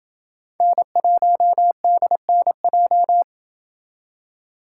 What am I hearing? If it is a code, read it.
Morse code, heard as a keyed tone, about 28 wpm, 700 Hz. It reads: N1DNJ